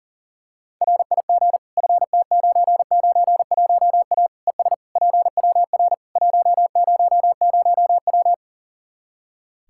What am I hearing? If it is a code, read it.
Morse code, heard as a keyed tone, about 40 wpm, 700 Hz. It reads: RIG FT991A ES PWR 100W
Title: Practice QSO - Exchange 2 - Info - 40wpm